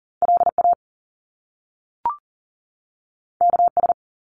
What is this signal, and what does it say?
Morse code, heard as a keyed tone, about 42 wpm, 700 Hz. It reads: LA KS
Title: US States 40wpm